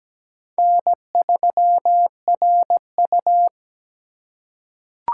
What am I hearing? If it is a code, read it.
Morse code, heard as a keyed tone, about 17 wpm, 700 Hz. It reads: N3RU
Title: USA Call Signs - 4 Characters - 2x Character Spacing - 17wpm